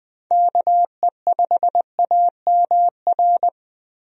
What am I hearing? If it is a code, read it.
Morse code, heard as a keyed tone, about 20 wpm, 700 Hz. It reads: KE5AMR
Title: USA Call Signs - 6 Characters - Repeated 3x - 20wpm